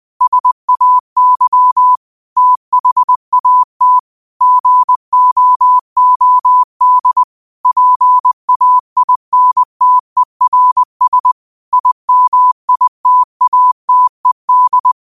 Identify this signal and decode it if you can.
Morse code, heard as a keyed tone, about 20 wpm, 1000 Hz. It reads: SAY THAT GOOD PAINTERS IMITATED
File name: qotdfiles / 20wpm